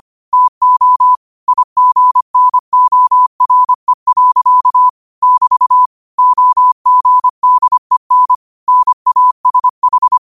Morse code, heard as a keyed tone, about 25 wpm, 1000 Hz. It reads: TO IGNORE. <BT> OGDEN NASH